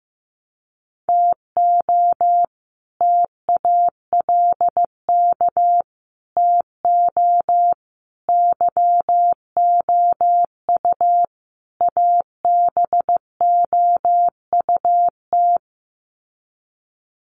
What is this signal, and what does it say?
Morse code, heard as a keyed tone, about 15 wpm, 700 Hz. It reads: TO TALK TO YOU ABOUT
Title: Sets of 5 Words Encore 15wpm